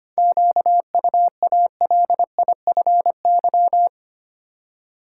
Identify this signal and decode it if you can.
Morse code, heard as a keyed tone, about 25 wpm, 700 Hz. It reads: QUALIFY
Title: Seven Letter Words - 25wpm